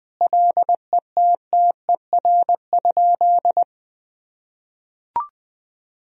Morse code, heard as a keyed tone, about 20 wpm, 700 Hz. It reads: LETTER?
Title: Sentences from Top 300 Words and Abbreviations 20wpm